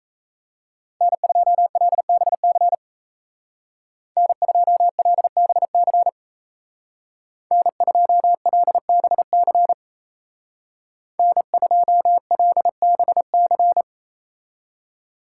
Morse code, heard as a keyed tone, about 33 wpm, 700 Hz. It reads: N2LBC N2LBC N2LBC N2LBC
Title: Speed Racing - USA 1x3 Call Signs 28wpm